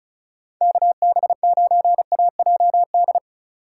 Morse code, heard as a keyed tone, about 34 wpm, 700 Hz. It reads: KB9AJD
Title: USA Call Signs - 2x Character Spacing - 35wpm